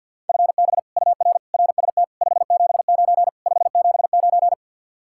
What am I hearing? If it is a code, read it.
Morse code, heard as a keyed tone, about 63 wpm, 700 Hz. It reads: FB UR RST 579 579